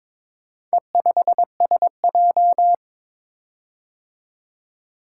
Morse code, heard as a keyed tone, about 22 wpm, 700 Hz. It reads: E5SJ
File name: practice / 22wpm / Four Letter-Number Combinations - 22wpm